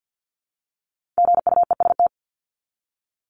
Morse code, heard as a keyed tone, about 50 wpm, 700 Hz. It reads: GUEST
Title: Top 1501-1750 Words 50wpm